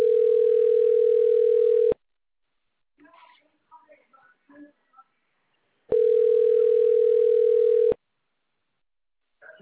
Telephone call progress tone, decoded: ringback tone